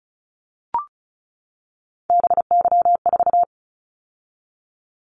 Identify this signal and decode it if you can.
Morse code, heard as a keyed tone, about 34 wpm, 700 Hz. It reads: BY4
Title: Callsign Prefixes for Top 100 DXCC - 35wpm